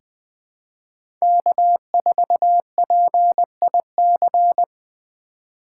Morse code, heard as a keyed tone, about 20 wpm, 700 Hz. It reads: K4PIC
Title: 2023 SOTA Chaser Call Signs - 20wpm